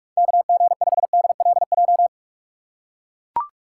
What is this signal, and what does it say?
Morse code, heard as a keyed tone, about 41 wpm, 700 Hz. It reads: KG5DLJ
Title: USA Call Signs - 6 Characters - 3x Character Spacing - 45wpm